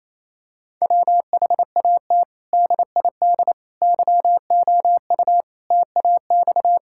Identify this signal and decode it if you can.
Morse code, heard as a keyed tone, about 28 wpm, 700 Hz. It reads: WHAT DID YOU TAX
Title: Sentences from Top 1000 Words - Review Part 5 - 28wpm